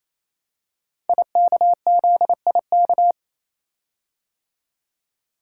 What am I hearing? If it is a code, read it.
Morse code, heard as a keyed tone, about 29 wpm, 690 Hz. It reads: IKZIK
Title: Code Groups - 5 Letters - 28wpm